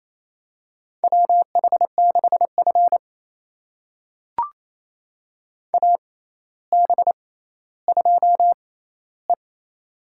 Morse code, heard as a keyed tone, about 28 wpm, spaced as for 14 wpm, 700 Hz. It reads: WH6F AB2E
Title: USA Call Signs - 4 Characters - 6x Character Spacing - 28wpm